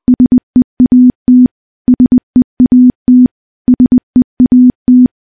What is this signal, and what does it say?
Morse code, heard as a keyed tone, about 20 wpm, 250 Hz. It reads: SEAT SEAT SEAT
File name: morse-training-20WPM